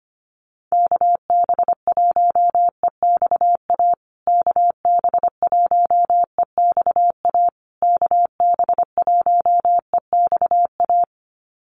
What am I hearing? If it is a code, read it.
Morse code, heard as a keyed tone, about 25 wpm, 700 Hz. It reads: KB1EXA KB1EXA KB1EXA
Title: USA Call Signs Repeated 3x 25wpm